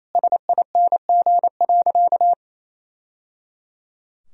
Morse code, heard as a keyed tone, about 28 wpm, 700 Hz. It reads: SING.
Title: Sentences from Top 1000 Words - Review Part 2 - 28wpm